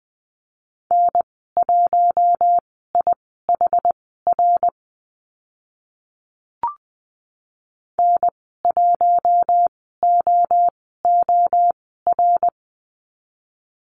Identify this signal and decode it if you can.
Morse code, heard as a keyed tone, about 20 wpm, spaced as for 15 wpm, 700 Hz. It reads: N1IHR N1OOR
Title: 076 Lesson 13g - Letter L - Call Signs 20wpm